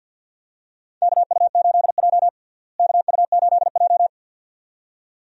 Morse code, heard as a keyed tone, about 52 wpm, 700 Hz. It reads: KU8J KU8J
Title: USA Call Signs - 4 Characters - Repeated 2x - 50wpm